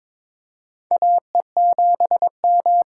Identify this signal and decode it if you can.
Morse code, heard as a keyed tone, about 21 wpm, 700 Hz. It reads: AE7M
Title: USA Call Signs - 5 Characters - 8x Character Spacing - 22wpm